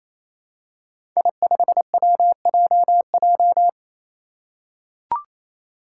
Morse code, heard as a keyed tone, about 29 wpm, 700 Hz. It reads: I5WJJ
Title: Five Challenging Letter-Number Combinations 28wpm